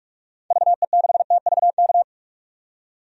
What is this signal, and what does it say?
Morse code, heard as a keyed tone, about 46 wpm, 700 Hz. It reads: VE6TVK
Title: DXCC-100 Call Signs - 6 Characters - 45wpm